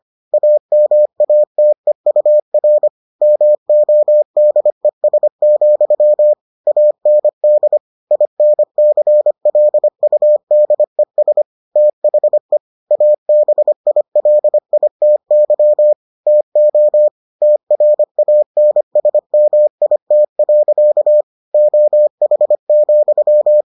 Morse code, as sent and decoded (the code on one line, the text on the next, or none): .- -- .- - . ..- .-. / -- --- -.. ... --..-- / .- -. -.. / .. -. -.-. .-.. ..- -.. ... / - .... / .- -... .. .-.. .. - -.-- / - --- / - .-. .- -. ... -- .. - .-.-.- / --- .... --..--
AMATEUR MODES, AND INCLUDES THE ABILITY TO TRANSMIT. OH,